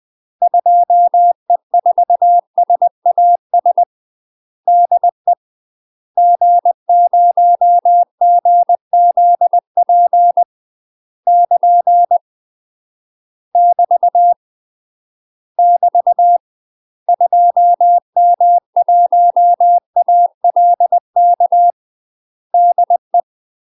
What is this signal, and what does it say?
Morse code, heard as a keyed tone, about 20 wpm, 700 Hz. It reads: 2E4SAS DE G0GZP <KN> <BT> <BT> 2M1ALK DE
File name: UK MASTER QSOs TRAINING / 20WPM / UK MASTER QSOs 0017